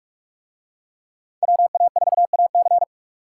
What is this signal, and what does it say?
Morse code, heard as a keyed tone, about 46 wpm, 700 Hz. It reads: WA4AC